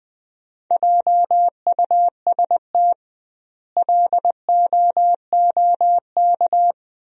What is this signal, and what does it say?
Morse code, heard as a keyed tone, about 20 wpm, 700 Hz. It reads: JUST LOOK
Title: 2x Word-Spacing - Sentences from Top 200 Words - 20wpm